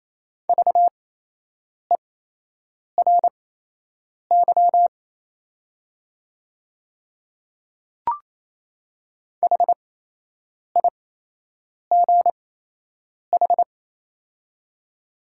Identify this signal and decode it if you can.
Morse code, heard as a keyed tone, about 28 wpm, spaced as for 8 wpm, 700 Hz. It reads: VERY HIGH
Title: 8x Character Spacing - Top 200 Words - 28wpm